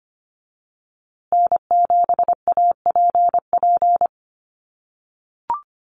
Morse code, heard as a keyed tone, about 25 wpm, 700 Hz. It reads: N7APP
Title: USA Call Signs Repeated 2x 25wpm